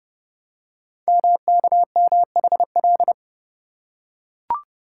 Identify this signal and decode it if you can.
Morse code, heard as a keyed tone, about 30 wpm, 700 Hz. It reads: MKMHL
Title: Code Groups - 5 Letters - 30wpm